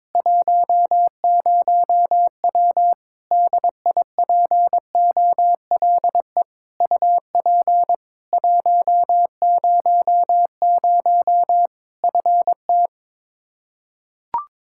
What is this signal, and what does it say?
Morse code, heard as a keyed tone, about 22 wpm, 700 Hz. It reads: 10W DIPOLE UP 100 FT
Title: Practice QSO - Exchange 2 - Info - 22wpm